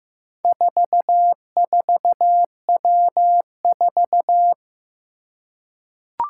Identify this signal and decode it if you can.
Morse code, heard as a keyed tone, about 15 wpm, 700 Hz. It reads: 44W4
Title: Four Challenging Letter-Number Combinations 15wpm